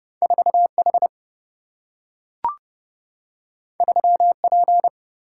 Morse code, heard as a keyed tone, about 30 wpm, 700 Hz. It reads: 4 H 3 P
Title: Single-Letter Single-Number 30wpm